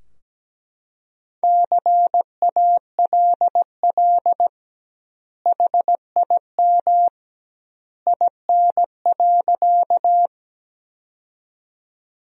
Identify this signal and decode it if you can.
Morse code, heard as a keyed tone, about 17 wpm, 700 Hz. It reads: CALL HIM IN.
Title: 2x Word-Spacing - Sentences from Top 100 Words - 17wpm